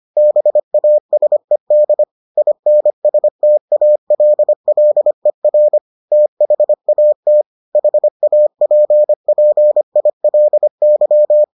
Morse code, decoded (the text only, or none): BASED INSTALLER THAT HAPPILY